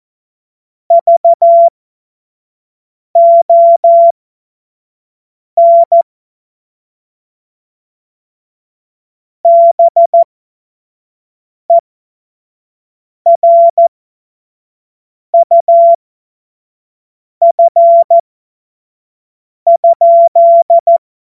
Morse code, heard as a keyed tone, about 14 wpm, spaced as for 5 wpm, 680 Hz. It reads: VON BERUF?
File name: Morsekurs / Witze / Witz 13. Januar 2026 25 F70